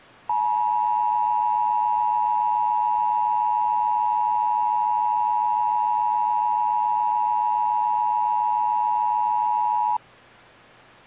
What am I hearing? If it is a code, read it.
Emergency Alert System (SAME) tones (the 853 and 960 Hz attention tone). It is heard from 0.2 to 10.0 s.
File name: eas / KBBX / KBBX